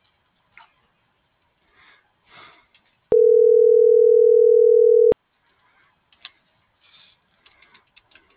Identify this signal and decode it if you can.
Ringback tone (North America), 440 + 480 Hz.
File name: CENTRAL MAYAB / out vianey.prado